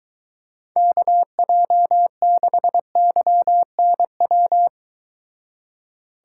Morse code, heard as a keyed tone, about 23 wpm, 700 Hz. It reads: KJ6YNW